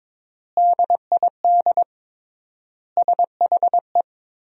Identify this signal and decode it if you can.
Morse code, heard as a keyed tone, about 21 wpm, 700 Hz. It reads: DID SHE
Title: 3x Word-Spacing - Sentences from Top 200 Words - 22wpm